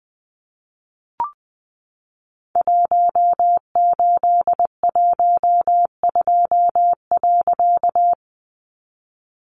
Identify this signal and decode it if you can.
Morse code, heard as a keyed tone, about 20 wpm, 700 Hz. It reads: 1812.
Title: A Sound Of Thunder 20wpm